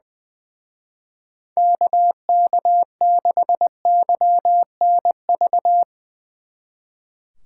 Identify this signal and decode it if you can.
Morse code, heard as a keyed tone, about 22 wpm, 700 Hz. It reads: KK6YNV
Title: Speed Racing - USA 2x3 Call Signs 20wpm